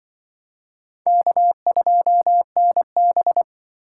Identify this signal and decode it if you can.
Morse code, heard as a keyed tone, about 25 wpm, 700 Hz. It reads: K2NB